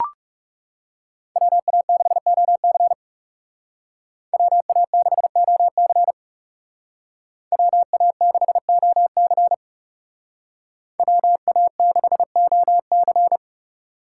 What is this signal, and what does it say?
Morse code, heard as a keyed tone, about 35 wpm, 700 Hz. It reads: WA6OC WA6OC WA6OC WA6OC
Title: Speed Racing - USA 2x2 Call Signs 30wpm